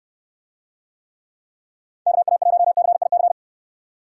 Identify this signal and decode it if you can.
Morse code, heard as a keyed tone, about 69 wpm, 700 Hz. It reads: KA1CIC